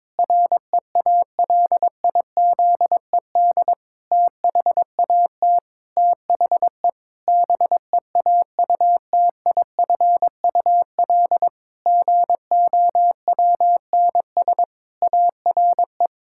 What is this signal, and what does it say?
Morse code, heard as a keyed tone, about 22 wpm, 700 Hz. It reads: REALIZED THAT THE BEAUTIFUL GOWNS ARE
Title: Pangram Challenge v2 - 22wpm